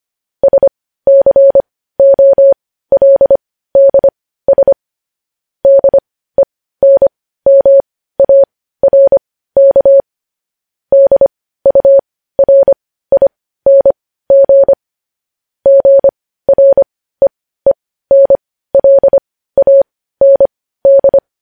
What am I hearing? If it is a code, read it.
Morse code, heard as a keyed tone, about 25 wpm, spaced as for 15 wpm, 550 Hz. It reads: SCOLDS DENMARK DURING GREENLAND
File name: podcast / News headlines-1x-25x15